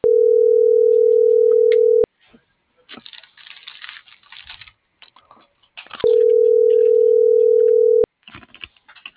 Ringback tone (North America), 440 + 480 Hz.